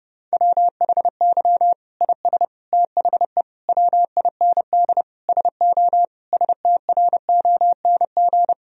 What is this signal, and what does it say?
Morse code, heard as a keyed tone, about 30 wpm, 700 Hz. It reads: WHY IS THE WIND SO STRONG
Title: Sentences from Top 1000 Words - Review Part 2 - 30wpm